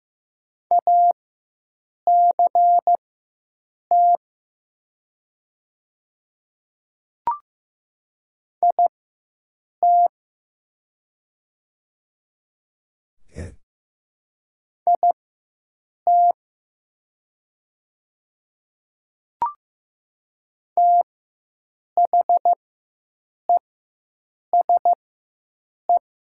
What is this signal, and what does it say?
Morse code, heard as a keyed tone, about 15 wpm, spaced as for 7 wpm, 700 Hz. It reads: ACT IT IT THESE
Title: 4x Character Spacing - Top 300 Words - 15wpm